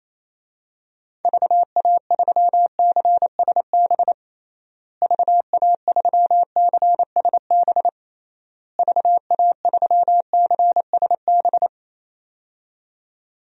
Morse code, heard as a keyed tone, about 28 wpm, 700 Hz. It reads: VA3CSB VA3CSB VA3CSB
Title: DXCC-100 Call Signs - 6 Characters - Repeated 3x - 28wpm